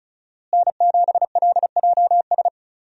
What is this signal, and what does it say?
Morse code, heard as a keyed tone, about 34 wpm, 700 Hz. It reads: N7LJS
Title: USA Call Signs - 5 Characters - Repeated 2x - 35wpm